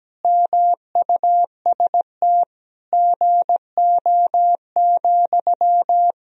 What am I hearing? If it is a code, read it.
Morse code, heard as a keyed tone, about 17 wpm, 700 Hz. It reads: MUST GO,
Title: Sentences from Top 1000 Words - Review Part 4 - 17wpm